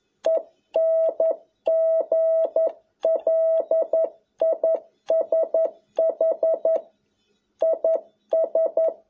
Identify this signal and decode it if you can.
Morse code, heard as a keyed tone, about 11 wpm, 650 Hz. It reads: ENGLISH IS
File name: jh8cbh / 2021cw